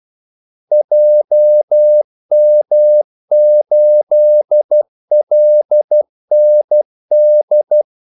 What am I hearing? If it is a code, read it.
Morse code, heard as a keyed tone, about 12 wpm, 600 Hz. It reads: JM8LND
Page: 2 ekstra ordmellemrum mellem kaldesignalerne